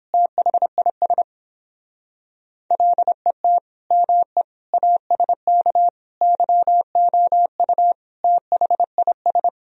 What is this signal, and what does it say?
Morse code, heard as a keyed tone, about 29 wpm, 700 Hz. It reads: THIS LET ME ASK YOU THIS